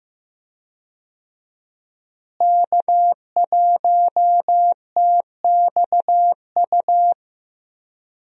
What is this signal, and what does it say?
Morse code, heard as a keyed tone, about 15 wpm, 700 Hz. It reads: K1TXU
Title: USA Call Signs - 5 Characters - Repeated 2x - 15wpm